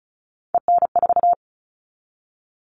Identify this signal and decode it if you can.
Morse code, heard as a keyed tone, about 34 wpm, 700 Hz. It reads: EN4
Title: All Cut Numbers - 3-Digit 35wpm